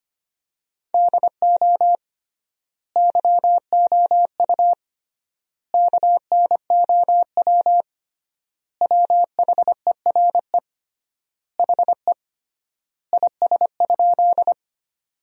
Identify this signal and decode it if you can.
Morse code, heard as a keyed tone, about 25 wpm, 700 Hz. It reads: DO YOU KNOW WHERE HE IS?
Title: 3x Word-Spacing - Sentences from Top 300 Words - 25wpm